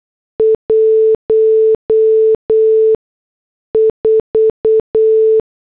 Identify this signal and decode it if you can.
Morse code, heard as a keyed tone, about 8 wpm, spaced as for 6 wpm, 440 Hz. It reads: 14